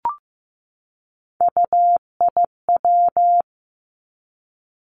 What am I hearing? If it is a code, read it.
Morse code, heard as a keyed tone, about 15 wpm, 700 Hz. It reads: UIW
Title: Three Challenging Letter-Number Combinations 15wpm